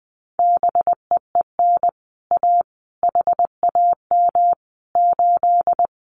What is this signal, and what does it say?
Morse code, heard as a keyed tone, about 20 wpm, 700 Hz. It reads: BEEN A HAM 8
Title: Common QSO Phrases Encore 20wpm